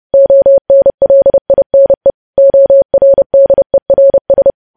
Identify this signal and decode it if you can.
Morse code, heard as a keyed tone, about 30 wpm, 550 Hz. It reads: ONLINE ORDERS
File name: podcast / News headlines-1x-30x30